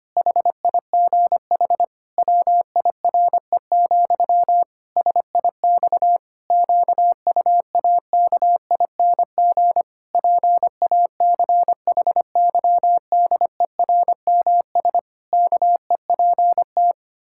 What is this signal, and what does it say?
Morse code, heard as a keyed tone, about 25 wpm, 700 Hz. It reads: HIGH WIRE, SIX QUAKING PACHYDERMS KEPT
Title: Pangram Challenge v2 - 25wpm